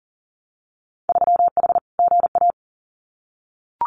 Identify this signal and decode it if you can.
Morse code, heard as a keyed tone, about 40 wpm, 700 Hz. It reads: 3H GA
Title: Winter Field Day Exchanges Encore 40wpm